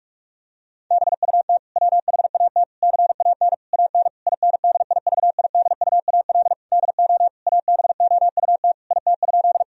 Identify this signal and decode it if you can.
Morse code, heard as a keyed tone, about 46 wpm, 700 Hz. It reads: BUT WHAT CAN AN INDIVIDUAL DO ABOUT IT?
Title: Sentences from Top 1000 Words - Review Part 4 - 45wpm